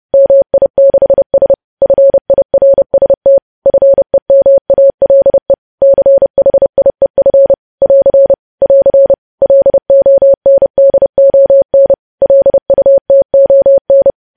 Morse code, heard as a keyed tone, about 30 wpm, 550 Hz. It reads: MI6S FIRST FEMALE CHIEF <AR> <AR> LONDON LUTON
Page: news headlines in morse code at 30 wpm